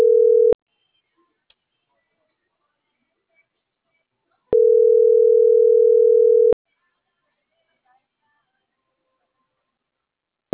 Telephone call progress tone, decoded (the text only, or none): ringback tone